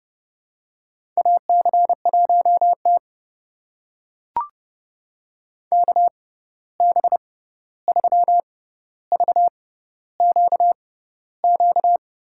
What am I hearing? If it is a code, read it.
Morse code, heard as a keyed tone, about 30 wpm, spaced as for 14 wpm, 700 Hz. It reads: AC1T KB3VQQ
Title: USA Call Signs - 6x Character Spacing - 30wpm